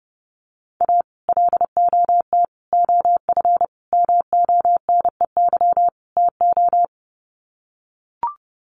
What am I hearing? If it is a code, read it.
Morse code, heard as a keyed tone, about 30 wpm, 700 Hz. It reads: A LOT OF MONEY TO
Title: Sets of 5 Words Encore2 30wpm